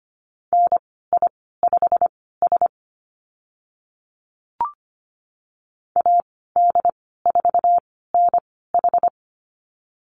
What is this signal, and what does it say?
Morse code, heard as a keyed tone, about 25 wpm, spaced as for 16 wpm, 700 Hz. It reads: NI5S AD4NH
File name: Lesson 15g - Number 5 - Call Signs 25wpm